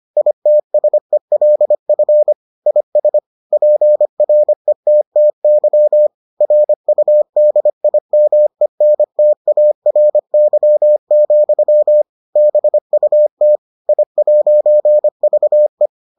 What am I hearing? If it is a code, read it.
Morse code, heard as a keyed tone, about 25 wpm, 600 Hz. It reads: ITSELF IS PRETTY RUDIMENTARY, BUT I'VE